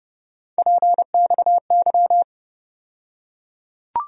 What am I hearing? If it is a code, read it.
Morse code, heard as a keyed tone, about 30 wpm, 700 Hz. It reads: PXY
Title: USA Call Sign Postfixes - 3 Letters 30wpm